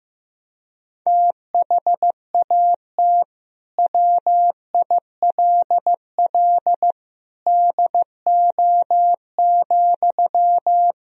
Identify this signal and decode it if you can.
Morse code, heard as a keyed tone, about 15 wpm, 700 Hz. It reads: THAT WILL DO,
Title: Sentences from Top 100 Words and Abbreviations 15wpm